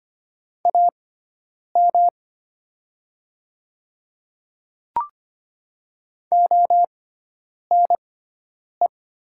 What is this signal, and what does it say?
Morse code, heard as a keyed tone, about 25 wpm, spaced as for 9 wpm, 700 Hz. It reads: AM ONE
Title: 6x Character Spacing - Top 500 Words - 25wpm